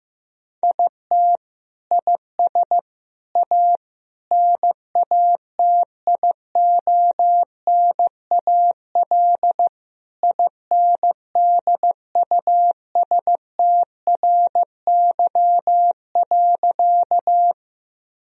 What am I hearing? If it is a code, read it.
Morse code, heard as a keyed tone, about 15 wpm, 700 Hz. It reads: IT IS A NATIONAL INDUSTRY.
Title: Sentences from Top 1000 Words - Review Part 4 - 15wpm